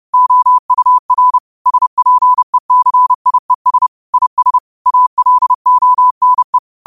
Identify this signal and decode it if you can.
Morse code, heard as a keyed tone, about 30 wpm, 1000 Hz. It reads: OUR SPECIES IS ALONE